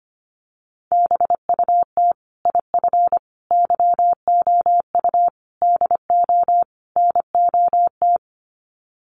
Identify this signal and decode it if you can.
Morse code, heard as a keyed tone, about 25 wpm, 700 Hz. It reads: BUT IF YOU DO NOT
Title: Sets of 5 Words Encore 25wpm